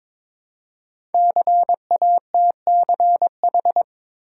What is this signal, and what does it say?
Morse code, heard as a keyed tone, about 22 wpm, 700 Hz. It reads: CATCH
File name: 5 Letter Words - First 3 Letters Given - 22wpm